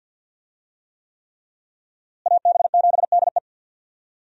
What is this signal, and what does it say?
Morse code, heard as a keyed tone, about 51 wpm, 700 Hz. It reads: AB7DE
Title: USA Call Signs - 5 Characters - Repeated 3x - 50wpm